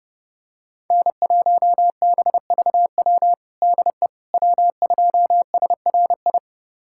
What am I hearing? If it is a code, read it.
Morse code, heard as a keyed tone, about 30 wpm, 700 Hz. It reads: N1BVW DE W2SRI
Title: USA Callsign DE Callsign - 5 Characters - 30wpm